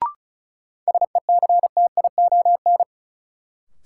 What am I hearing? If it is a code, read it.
Morse code, heard as a keyed tone, about 36 wpm, 700 Hz. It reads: SECTION
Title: Sentences from Top 1000 Words - Review Part 5 - New Word Drill - 35wpm